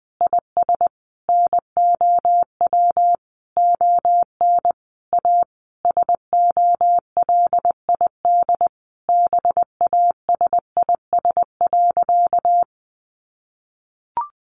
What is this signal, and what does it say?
Morse code, heard as a keyed tone, about 20 wpm, 700 Hz. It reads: IS NOW ON A SOLID BASIS.
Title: Sentences from Top 1500 Words 20wpm